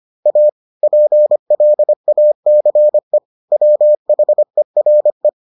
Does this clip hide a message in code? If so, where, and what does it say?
Yes: Morse code from 0.3 to 5.3 s, heard as a keyed tone, about 25 wpm, 600 Hz. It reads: A PLACE WHERE